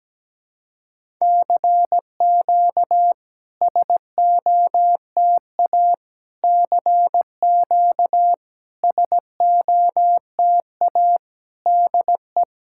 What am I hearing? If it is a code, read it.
Morse code, heard as a keyed tone, about 17 wpm, 690 Hz. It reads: CQ SOTA CQ SOTA DE
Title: SOTA QSO Practice 17wpm